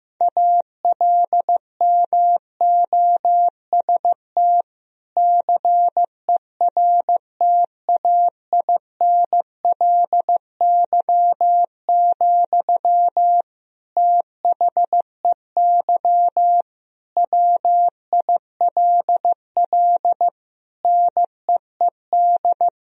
Morse code, heard as a keyed tone, about 15 wpm, 700 Hz. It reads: ALMOST CERTAINLY, THEY WILL NEED
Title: Sentences from Top 1000 Words - Review Part 4 - 15wpm